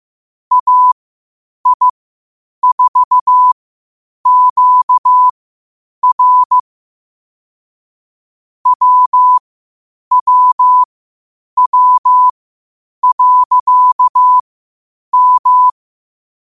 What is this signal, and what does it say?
Morse code, heard as a keyed tone, about 15 wpm, spaced as for 8 wpm, 1000 Hz. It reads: AI4QR WWW.M
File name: qotdfiles / 8wpm